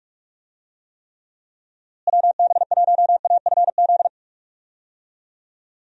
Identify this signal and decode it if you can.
Morse code, heard as a keyed tone, about 46 wpm, 700 Hz. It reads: WB1AFZ
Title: USA Call Signs - 6 Characters - Repeated 3x - 45wpm